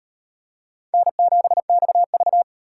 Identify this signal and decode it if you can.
Morse code, heard as a keyed tone, about 38 wpm, 700 Hz. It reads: N7XV